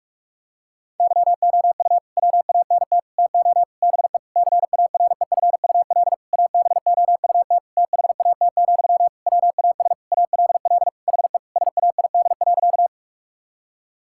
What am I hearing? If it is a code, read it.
Morse code, heard as a keyed tone, about 46 wpm, 700 Hz. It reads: YOU WANT TO BE CAREFUL ABOUT THAT, WAS ALL HE SAID.
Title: Sentences from Top 1600 Words 45wpm